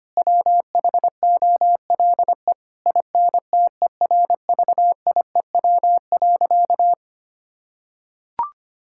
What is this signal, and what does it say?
Morse code, heard as a keyed tone, about 25 wpm, 700 Hz. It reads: WHOLE INTERVIEW.
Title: Sentences from Top 1600 Words 25wpm